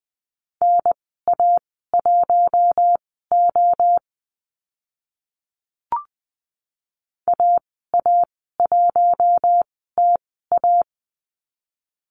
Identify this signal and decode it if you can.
Morse code, heard as a keyed tone, about 20 wpm, spaced as for 14 wpm, 700 Hz. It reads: NA1O AA1TA
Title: 042 Lesson 8g - Number 1 - Call Signs 20wpm